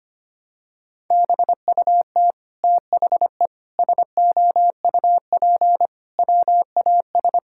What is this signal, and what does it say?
Morse code, heard as a keyed tone, about 25 wpm, 700 Hz. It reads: BUT THE SOUP WAS
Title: Sentences from Top 1000 Words - Review Part 6 - 25wpm